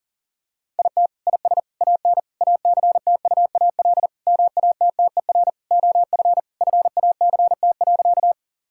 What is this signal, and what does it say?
Morse code, heard as a keyed tone, about 37 wpm, 700 Hz. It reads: IT IS AN ACTUAL MATTER OF FACT.
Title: Sentences from Top 1000 Words - Review Part 5 - 40wpm